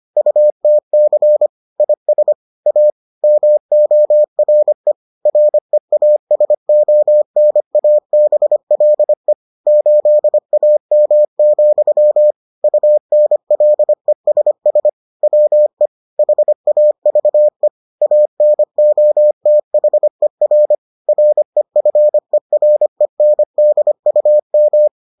Morse code, heard as a keyed tone, about 25 wpm, 600 Hz. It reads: UTC IS A MORE REASONABLE 8AM, UNLESS WE HAVE ANOTHER REFERENDUM